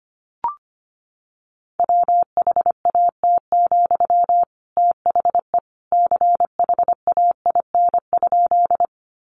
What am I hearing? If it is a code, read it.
Morse code, heard as a keyed tone, about 25 wpm, 700 Hz. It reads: WHAT, THE CHAIN?
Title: Sentences from Top 1500 Words 25wpm